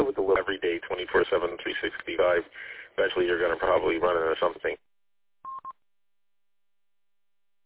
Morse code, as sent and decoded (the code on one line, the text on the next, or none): none